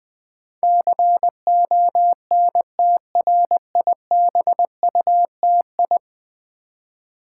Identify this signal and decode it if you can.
Morse code, heard as a keyed tone, about 19 wpm, 700 Hz. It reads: CONTRIBUTI
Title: Build A Word - 5 Letters At A Time - 12-Letter Words - 20wpm